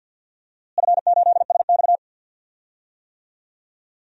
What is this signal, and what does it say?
Morse code, heard as a keyed tone, about 50 wpm, 700 Hz. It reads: F8SX
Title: DXCC-100 Call Signs - 4 Characters - Repeated 2x - 50wpm